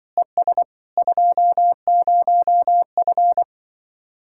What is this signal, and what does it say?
Morse code, heard as a keyed tone, about 24 wpm, 700 Hz. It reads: ES 20F
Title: Speed-Racing Common QSO Phrases - 20wpm